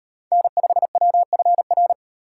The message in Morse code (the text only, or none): N5WFR